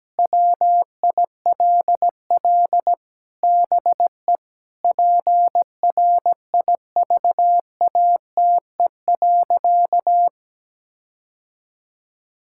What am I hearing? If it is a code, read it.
Morse code, heard as a keyed tone, about 17 wpm, 700 Hz. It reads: WILL BE PRIVATE.
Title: Sentences from Top 1600 Words 17wpm